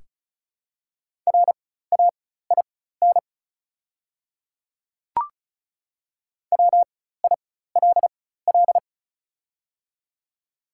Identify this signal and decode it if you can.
Morse code, heard as a keyed tone, about 36 wpm, spaced as for 16 wpm, 700 Hz. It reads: RAIN WILL
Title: 4x Character Spacing - Top 300 Words - 35wpm